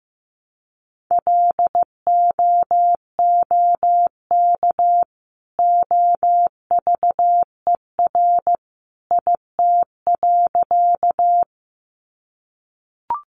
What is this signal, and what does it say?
Morse code, heard as a keyed tone, about 15 wpm, 700 Hz. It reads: LOOK OVER IT.
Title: Sentences from Top 100 Words 15wpm